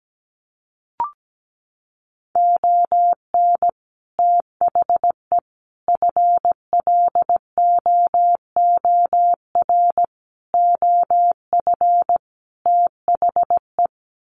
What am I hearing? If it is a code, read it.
Morse code, heard as a keyed tone, about 17 wpm, 700 Hz. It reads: ON THE FLOOR OF THE
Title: Sets of 5 Words 17wpm